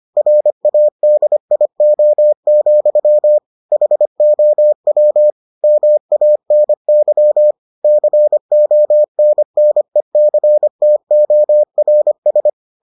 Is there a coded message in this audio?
Yes: Morse code from 0.2 to 12.5 s, heard as a keyed tone, about 25 wpm, 600 Hz. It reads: RADIO, HOW MANY CONNECTORS